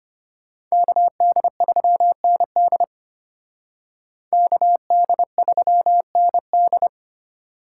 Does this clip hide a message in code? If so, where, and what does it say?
Yes: Morse code from 0.7 to 6.9 s, heard as a keyed tone, about 33 wpm, 700 Hz. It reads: KD3ND KD3ND